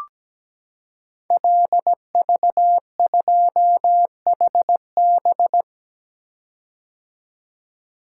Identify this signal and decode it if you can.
Morse code, heard as a keyed tone, about 17 wpm, 700 Hz. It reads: LV2HB
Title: Code Groups - 5 Letter:Numbers - 17wpm